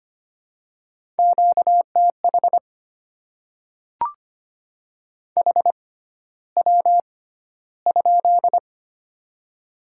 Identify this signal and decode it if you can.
Morse code, heard as a keyed tone, about 25 wpm, spaced as for 13 wpm, 700 Hz. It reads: QTH HW?
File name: Top 20 QSO Elements - 6x Character Spacing - 25wpm